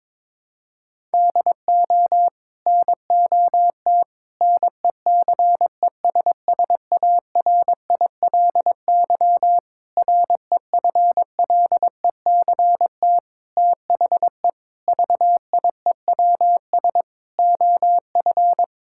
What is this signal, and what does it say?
Morse code, heard as a keyed tone, about 21 wpm, 700 Hz. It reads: DO NOT NECESSARILY REFLECT THE VIEWS OF
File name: practice / 22wpm / Sets of 8 Words - 22wpm